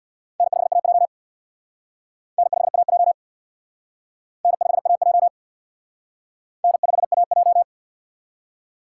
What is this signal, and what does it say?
Morse code, heard as a keyed tone, about 60 wpm, 700 Hz. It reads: N5AJ N5AJ N5AJ N5AJ
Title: Speed Racing - USA 1x2 Call Signs 50wpm